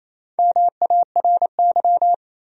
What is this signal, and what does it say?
Morse code, heard as a keyed tone, about 28 wpm, 700 Hz. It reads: MARY
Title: USA Common Names - Max 4 Letters - 28wpm